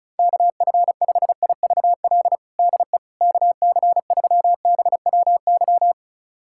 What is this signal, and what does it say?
Morse code, heard as a keyed tone, about 34 wpm, 700 Hz. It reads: KF5IVL DE KC3BWY
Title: USA Callsign DE Callsign - 6 Characters - 35wpm